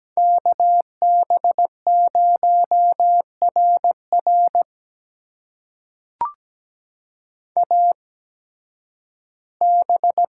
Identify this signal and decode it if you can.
Morse code, heard as a keyed tone, about 17 wpm, spaced as for 11 wpm, 700 Hz. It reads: KB0RR AB
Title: USA Call Signs - 5 Characters - 8x Character Spacing - 17wpm